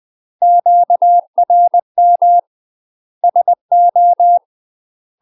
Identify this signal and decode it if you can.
Morse code, heard as a keyed tone, about 20 wpm, 700 Hz. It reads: QRM SO
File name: UK MASTER QSOs TRAINING / 20WPM / UK MASTER QSOs 0071